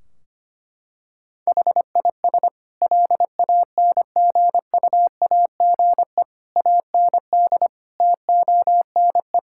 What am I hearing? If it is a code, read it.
Morse code, heard as a keyed tone, about 25 wpm, 700 Hz. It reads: HIS LANGUAGE AND TONE
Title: Sentences from Top 1600 Words 25wpm